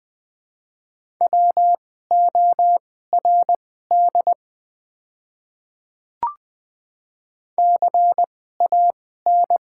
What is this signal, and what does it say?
Morse code, heard as a keyed tone, about 20 wpm, spaced as for 14 wpm, 700 Hz. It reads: WORD CAN
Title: 2x Character Spacing - Top 100 Words - 20wpm